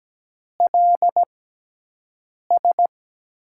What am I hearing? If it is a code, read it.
Morse code, heard as a keyed tone, about 17 wpm, spaced as for 6 wpm, 700 Hz. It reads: LS
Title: USA Call Signs - 6x Character Spacing - 17wpm